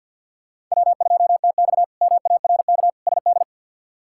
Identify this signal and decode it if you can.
Morse code, heard as a keyed tone, about 53 wpm, 700 Hz. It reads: W2TX MARK SD
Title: K1USN SST QSO Practice - 50wpm